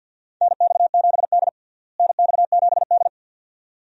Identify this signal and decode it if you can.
Morse code, heard as a keyed tone, about 46 wpm, 700 Hz. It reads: NX7D NX7D
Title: USA Call Signs - 4 Characters - Repeated 2x - 50wpm